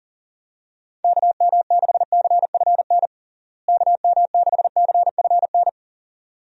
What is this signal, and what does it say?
Morse code, heard as a keyed tone, about 40 wpm, 700 Hz. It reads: KM6CFN KM6CFN
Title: USA Call Signs - 6 Characters - Repeated 2x - 40wpm